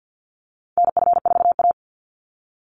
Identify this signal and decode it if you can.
Morse code, heard as a keyed tone, about 50 wpm, 700 Hz. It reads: NF4A
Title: Call Signs - 50wpm